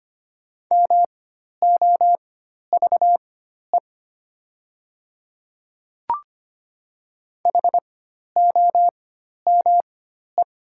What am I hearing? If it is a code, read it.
Morse code, heard as a keyed tone, about 25 wpm, spaced as for 12 wpm, 700 Hz. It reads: MOVE HOME
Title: 4x Character Spacing - Top 200 Words - 25wpm